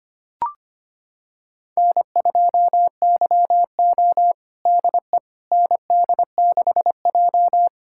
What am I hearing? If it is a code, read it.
Morse code, heard as a keyed tone, about 25 wpm, 700 Hz. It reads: N2YO DE ND6J
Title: USA Callsign DE Callsign - 4 Characters - 25wpm